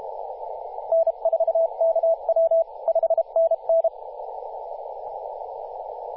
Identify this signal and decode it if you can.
Morse code, heard as a keyed tone, about 32 wpm, 660 Hz, with radio noise. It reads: N4KW 5NN
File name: posts / 2016-oct-05-2124z-t31t-18-072mhz-cw